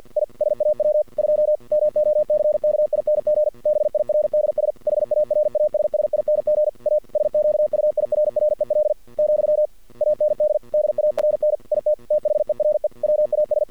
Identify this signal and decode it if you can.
Morse code, heard as a keyed tone, about 50 wpm, 610 Hz. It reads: I SAW 2 MOCKING BIRDS HARASSING A SQUIRREL , AND DARN IT THERE WAS
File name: archived / CRSnet